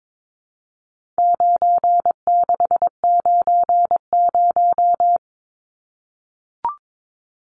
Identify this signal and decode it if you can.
Morse code, heard as a keyed tone, about 22 wpm, 700 Hz. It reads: 9690
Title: Numbers - 4-Digit 22wpm